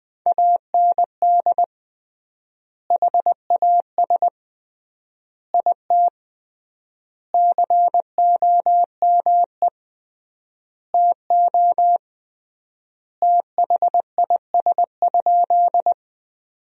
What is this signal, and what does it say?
Morse code, heard as a keyed tone, about 20 wpm, 700 Hz. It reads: AND HAS IT COME TO THIS?
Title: 3x Word-Spacing - Sentences from Top 100 Words - 20wpm